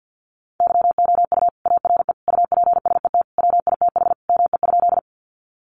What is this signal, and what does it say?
Morse code, heard as a keyed tone, about 50 wpm, 700 Hz. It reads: YOU ARE UPSET WITH ME?
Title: Sentences from Top 800 Words 50wpm